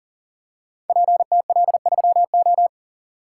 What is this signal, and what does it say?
Morse code, heard as a keyed tone, about 40 wpm, 700 Hz. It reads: PTL3O
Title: Code Groups - 5 Letter:Numbers - 40wpm